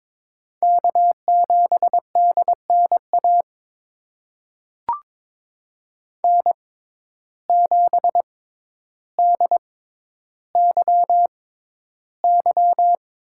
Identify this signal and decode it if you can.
Morse code, heard as a keyed tone, about 22 wpm, spaced as for 11 wpm, 700 Hz. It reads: K7DNA N7DYY
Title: USA Call Signs - 5 Characters - 6x Character Spacing - 22wpm